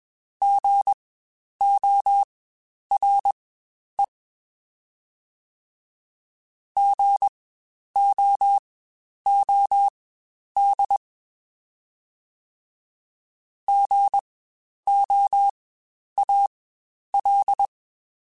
Morse code, heard as a keyed tone, about 21 wpm, spaced as for 10 wpm, 800 Hz. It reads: GORE GOOD GOAL
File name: Spaced Alphas / Spaced Alphas mp3s / Gs sp @20